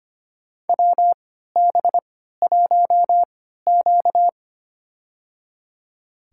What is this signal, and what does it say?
Morse code, heard as a keyed tone, about 25 wpm, spaced as for 14 wpm, 700 Hz. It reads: WB1Q
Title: USA Call Signs - 4 Characters - 3x Character Spacing - 25wpm